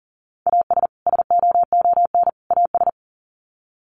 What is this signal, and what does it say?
Morse code, heard as a keyed tone, about 40 wpm, 700 Hz. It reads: AS SOON AS
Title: Sets of 3 Words 40wpm